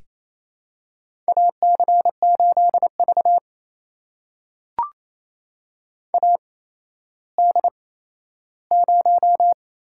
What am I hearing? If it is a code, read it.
Morse code, heard as a keyed tone, about 28 wpm, spaced as for 14 wpm, 700 Hz. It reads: AC8V AD0
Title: USA Call Signs - 8x Character Spacing - 28wpm